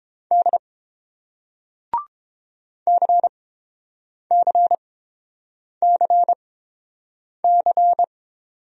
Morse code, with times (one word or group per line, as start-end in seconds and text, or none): 0.3-0.6 D
2.9-8.0 CCCC